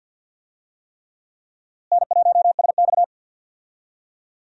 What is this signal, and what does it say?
Morse code, heard as a keyed tone, about 46 wpm, 700 Hz. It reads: N1SX
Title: USA Call Signs - 4 Characters - Repeated 3x - 50wpm